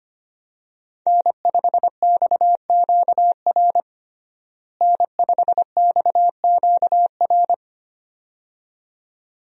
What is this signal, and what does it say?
Morse code, heard as a keyed tone, about 25 wpm, 700 Hz. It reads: N5XQR N5XQR
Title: USA Call Signs - 5 Characters - Repeated 2x - 25wpm